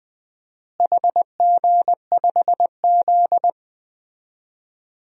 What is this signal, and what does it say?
Morse code, heard as a keyed tone, about 20 wpm, 700 Hz. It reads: HG5Z
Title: DXCC-100 Call Signs - 4 Characters - Repeated 2x - 20wpm